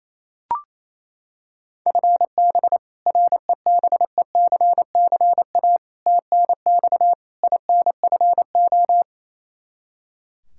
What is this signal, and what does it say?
Morse code, heard as a keyed tone, about 28 wpm, 700 Hz. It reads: FB REBECCA TNX INFO
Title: Practice QSO - Exchange 3 - More Info - 28wpm